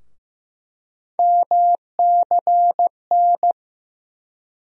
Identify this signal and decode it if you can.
Morse code, heard as a keyed tone, about 15 wpm, 700 Hz. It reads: MCN
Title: USA Call Sign Postfixes - 3 Letters 15wpm